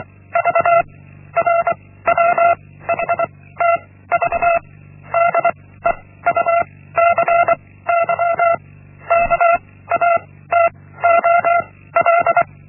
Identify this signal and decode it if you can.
Morse code, heard as a keyed tone, about 24 wpm, 680 Hz. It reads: V R W H T V D E U C Y K A T O L
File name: Linksmakalnis-1